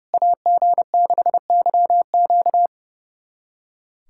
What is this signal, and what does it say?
Morse code, heard as a keyed tone, about 32 wpm, 700 Hz. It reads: AG6YQ
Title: USA Callsign DE Callsign 30wpm